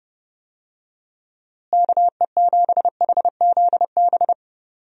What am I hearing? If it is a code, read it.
Morse code, heard as a keyed tone, about 32 wpm, 700 Hz. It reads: KE7HZB
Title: USA Call Signs - 6 Characters - Repeated 2x - 30wpm